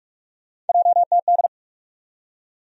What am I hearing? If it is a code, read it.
Morse code, heard as a keyed tone, about 47 wpm, 700 Hz. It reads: JTD
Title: USA Call Sign Postfixes - 3 Letters 45wpm